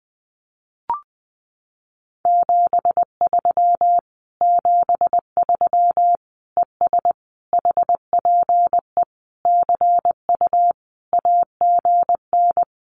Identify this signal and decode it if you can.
Morse code, heard as a keyed tone, about 20 wpm, 700 Hz. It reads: 73 73 ES HPE CU AGN
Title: Common QSO Phrases Encore 20wpm